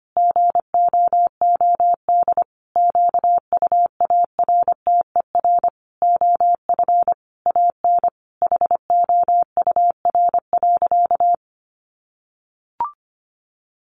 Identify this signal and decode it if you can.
Morse code, heard as a keyed tone, about 25 wpm, 700 Hz. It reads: GOOD QUARTER OF AN HOUR.
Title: Sentences from Top 1300 Words 25wpm